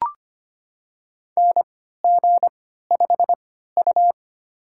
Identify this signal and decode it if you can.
Morse code, heard as a keyed tone, about 25 wpm, spaced as for 14 wpm, 700 Hz. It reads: NG5U
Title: USA Call Signs - 4 Characters - 3x Character Spacing - 25wpm